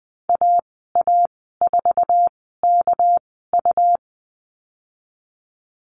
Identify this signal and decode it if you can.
Morse code, heard as a keyed tone, about 20 wpm, spaced as for 14 wpm, 700 Hz. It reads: AA4KU
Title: 219 Lesson 35g - Number 0 - Call Signs 20wpm